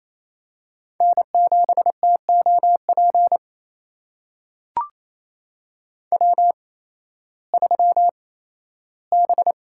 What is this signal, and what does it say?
Morse code, heard as a keyed tone, about 28 wpm, spaced as for 15 wpm, 700 Hz. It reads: N7TOP W3B
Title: USA Call Signs - 5 Characters - 8x Character Spacing - 28wpm